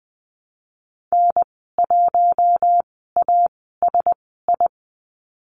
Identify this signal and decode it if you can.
Morse code, heard as a keyed tone, about 20 wpm, spaced as for 15 wpm, 700 Hz. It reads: N1ASI
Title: 076 Lesson 13g - Letter L - Call Signs 20wpm